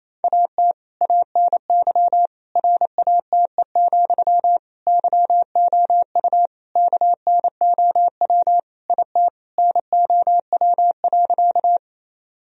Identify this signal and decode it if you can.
Morse code, heard as a keyed tone, about 28 wpm, 700 Hz. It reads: AT ANY RATE, YOU KNOW IT NOW.
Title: Sentences from Top 1000 Words - Review Part 4 - 28wpm